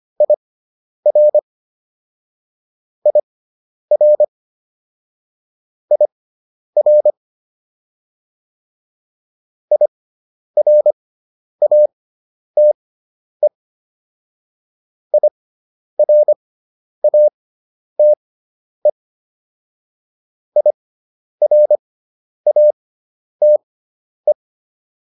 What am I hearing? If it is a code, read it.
Morse code, heard as a keyed tone, about 25 wpm, spaced as for 10 wpm, 600 Hz. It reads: IR IR IR IRATE IRATE IRATE